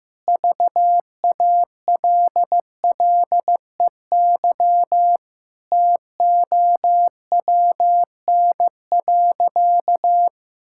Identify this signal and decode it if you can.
Morse code, heard as a keyed tone, about 15 wpm, 700 Hz. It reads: VALLEY TOWN.
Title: Pangram Challenge v2 - 15wpm